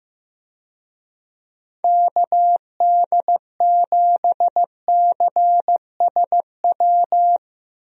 Morse code, heard as a keyed tone, about 16 wpm, 700 Hz. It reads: KD7CSW
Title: USA Call Signs - 6 Characters - Repeated 3x - 15wpm